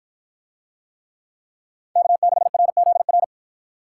Morse code, heard as a keyed tone, about 53 wpm, 700 Hz. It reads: K6RZR